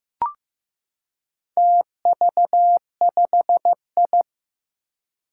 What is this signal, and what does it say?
Morse code, heard as a keyed tone, about 15 wpm, 700 Hz. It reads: TV5I
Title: Four Challenging Letter-Number Combinations 15wpm